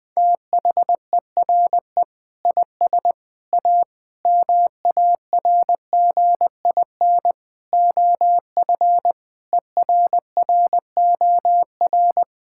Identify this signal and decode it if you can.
Morse code, heard as a keyed tone, about 20 wpm, 700 Hz. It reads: THERE IS A MARGIN OF ERROR
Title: Sentences with 8-Word Phrases - 20wpm